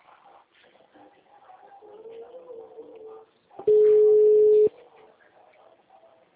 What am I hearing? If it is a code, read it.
Ringback tone (Europe, 425 Hz).